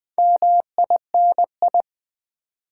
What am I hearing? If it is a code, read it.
Morse code, heard as a keyed tone, about 20 wpm, 700 Hz. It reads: MINI
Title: Build A Word - 2 Letters At A Time - 8-Letter Words - 20wpm